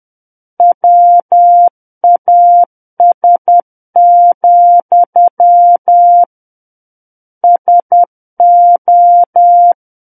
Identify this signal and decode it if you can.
Morse code, heard as a keyed tone, about 10 wpm, 700 Hz. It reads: WAS, SO
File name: Book One / 01-Chapter 1 - 10 WPM